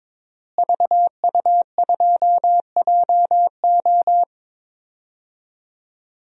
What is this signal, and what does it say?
Morse code, heard as a keyed tone, about 22 wpm, 700 Hz. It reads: VU2JO
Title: DXCC-100 Call Signs - 5 Characters - Repeated 3x - 22wpm